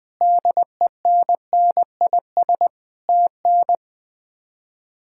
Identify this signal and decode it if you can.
Morse code, heard as a keyed tone, about 20 wpm, 700 Hz. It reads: DENNIS TN
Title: K1USN SST QSO Practice - 20wpm